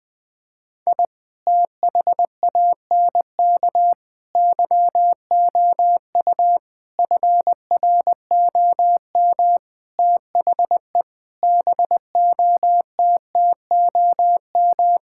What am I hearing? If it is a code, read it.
Morse code, heard as a keyed tone, about 20 wpm, 700 Hz. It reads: I THANK YOU FROM THE BOTTOM
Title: Sentences with 6-Word Phrases - Spoken Beginning - 20wpm